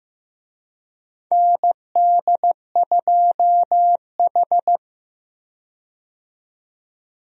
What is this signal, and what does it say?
Morse code, heard as a keyed tone, about 15 wpm, 700 Hz. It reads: ND2H
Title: USA Call Signs - 4 Characters - 15wpm